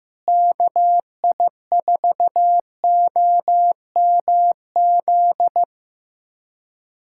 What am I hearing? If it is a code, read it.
Morse code, heard as a keyed tone, about 15 wpm, 700 Hz. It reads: KI4OMZ
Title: USA Callsign DE Callsign 15wpm